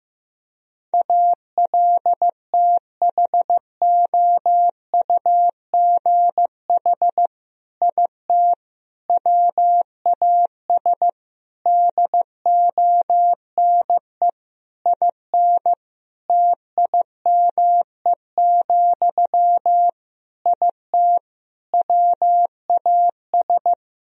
Morse code, heard as a keyed tone, about 15 wpm, 700 Hz. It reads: ALTHOUGH IT WAS DONE IN TIME, IT WAS
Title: Sentences from Top 1000 Words - Review Part 3 - 15wpm